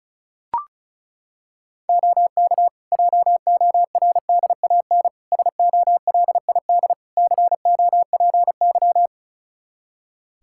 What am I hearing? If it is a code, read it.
Morse code, heard as a keyed tone, about 36 wpm, 700 Hz. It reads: OK JORDAN SOLID COPY
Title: Practice QSO - Exchange 3 - More Info - 35wpm